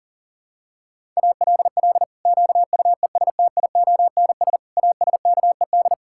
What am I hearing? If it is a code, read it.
Morse code, heard as a keyed tone, about 40 wpm, 700 Hz. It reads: ALL QUESTIONS ASKED
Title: Pangram Challenge v2 - 40wpm